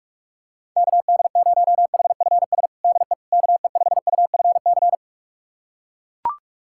Morse code, heard as a keyed tone, about 42 wpm, 700 Hz. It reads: KD0HFS DE KE5UFC
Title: USA Callsign DE Callsign - 6 Characters - 45wpm